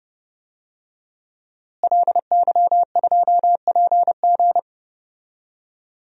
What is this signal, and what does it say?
Morse code, heard as a keyed tone, about 31 wpm, 700 Hz. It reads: LY2PG
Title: DXCC-100 Call Signs - 5 Characters - Repeated 2x - 30wpm